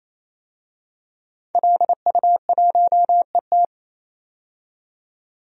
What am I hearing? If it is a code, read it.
Morse code, heard as a keyed tone, about 28 wpm, 700 Hz. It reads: LU1ET
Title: DXCC-100 Call Signs - 5 Characters - 28wpm